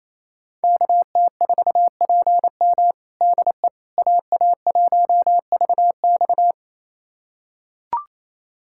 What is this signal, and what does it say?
Morse code, heard as a keyed tone, about 28 wpm, 700 Hz. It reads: KT4PM DE AA1VX
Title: USA Callsign DE Callsign - 5 Characters - 28wpm